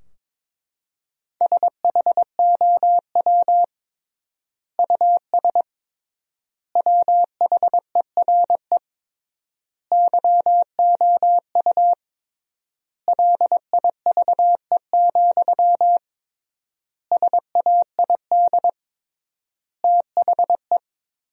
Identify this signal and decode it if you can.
Morse code, heard as a keyed tone, about 22 wpm, 700 Hz. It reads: SHOW US WHERE YOU LIVE, SAID THE
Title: 3x Word-Spacing - Sentences from Top 200 Words - 22wpm